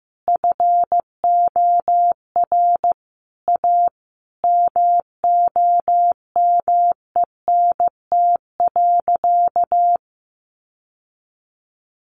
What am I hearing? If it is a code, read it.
Morse code, heard as a keyed tone, about 15 wpm, 700 Hz. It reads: FOR A MOMENT.
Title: Sentences from Top 1000 Words 15wpm